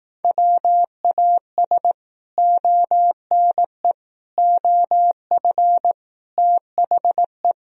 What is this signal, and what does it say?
Morse code, heard as a keyed tone, about 18 wpm, 700 Hz. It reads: WAS ONE OF THE
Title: Speed-Racing - Sets of 5 Words 15wpm